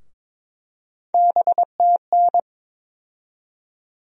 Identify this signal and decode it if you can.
Morse code, heard as a keyed tone, about 22 wpm, 700 Hz. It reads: BTN
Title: USA Call Sign Postfixes - 3 Letters 22wpm